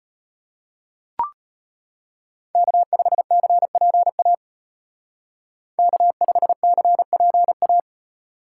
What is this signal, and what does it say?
Morse code, heard as a keyed tone, about 30 wpm, 700 Hz. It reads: K5CPA K5CPA
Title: Speed Racing - USA Call Signs 25wpm